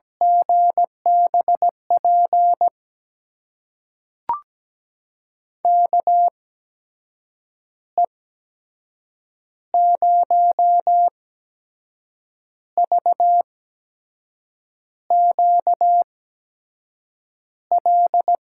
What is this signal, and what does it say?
Morse code, heard as a keyed tone, about 17 wpm, spaced as for 6 wpm, 700 Hz. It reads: GBP KE0VQL
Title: USA Call Signs - 6 Characters - 8x Character Spacing - 17wpm